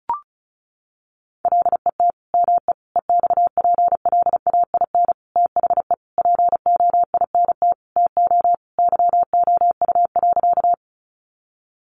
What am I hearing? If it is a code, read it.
Morse code, heard as a keyed tone, about 34 wpm, 700 Hz. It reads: LET ME EXPLAIN THE POINT TO YOU.
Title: Sentences from Top 1100 Words 35wpm